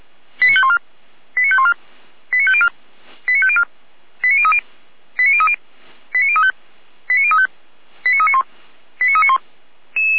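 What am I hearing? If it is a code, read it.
Five-tone selective call (ZVEI), heard as quick runs of five tones, tones of about 70 ms. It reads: 89514 89514 89504 89504 89030 89030 89035 89035 89391 89391
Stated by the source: ZVEI five-tone selective call 89140 received on SDR, from the Feuerwehr Orlamünde fire service audio archive.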